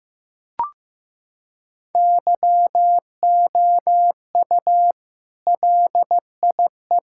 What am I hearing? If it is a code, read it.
Morse code, heard as a keyed tone, about 15 wpm, 700 Hz. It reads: YOU LIE
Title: Sentences from Top 1600 Words 15wpm